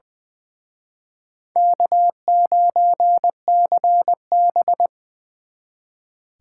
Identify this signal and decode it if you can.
Morse code, heard as a keyed tone, about 24 wpm, 700 Hz. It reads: K9CB
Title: Speed Racing - USA Call Signs 20wpm